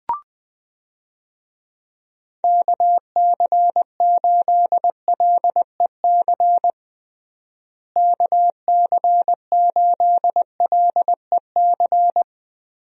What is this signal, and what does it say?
Morse code, heard as a keyed tone, about 20 wpm, 700 Hz. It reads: KC8LEC KC8LEC
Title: USA Call Signs - 6 Characters - Repeated 3x - 20wpm